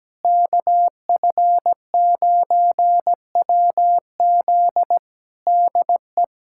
Morse code, heard as a keyed tone, about 17 wpm, 700 Hz. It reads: KF9WZ DE
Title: K1USN SST QSO Practice 17wpm